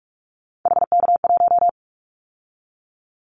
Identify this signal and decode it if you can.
Morse code, heard as a keyed tone, about 44 wpm, 700 Hz. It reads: HK1
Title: Callsign Prefixes for Top 100 DXCC - 45wpm